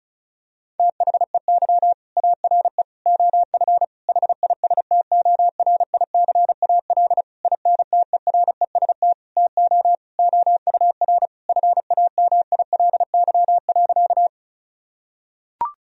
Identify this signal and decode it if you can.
Morse code, heard as a keyed tone, about 34 wpm, 700 Hz. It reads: THEY ARE OF HISTORICAL INTEREST TO OUR FAMILY.
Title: Sentences from Top 1000 Words - Review Part 4 - 35wpm